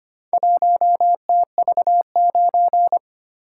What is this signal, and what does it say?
Morse code, heard as a keyed tone, about 26 wpm, 700 Hz. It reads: 1TV9
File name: practice / 25wpm / Four Letter-Number Combinations - 25wpm